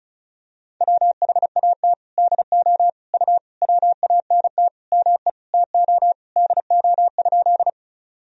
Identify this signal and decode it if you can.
Morse code, heard as a keyed tone, about 36 wpm, 700 Hz. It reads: WHAT DO U WANT ME TO DO?
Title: Sentences from Top 200 Words and Abbreviations 35wpm